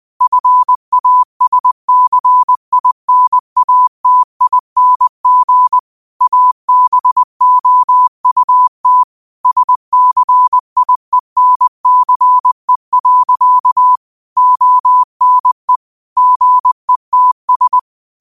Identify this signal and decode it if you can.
Morse code, heard as a keyed tone, about 20 wpm, 1000 Hz. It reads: FASCINATING ABOUT SCIENCE. ONE GETS
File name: qotdfiles / 20wpm